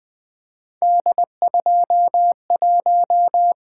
Morse code, heard as a keyed tone, about 20 wpm, 700 Hz. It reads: D21
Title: Three Letter-Number Combinations - 20wpm